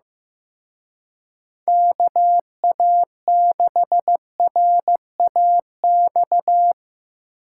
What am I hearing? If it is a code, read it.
Morse code, heard as a keyed tone, about 15 wpm, 700 Hz. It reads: KA6RAX
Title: Speed Racing - USA 2x3 Call Signs 15wpm